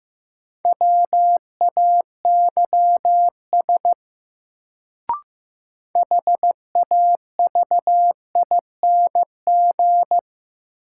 Morse code, heard as a keyed tone, about 15 wpm, 700 Hz. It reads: WAYS HAVING
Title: Sets of 5 Words Encore - New Word Drill - 15wpm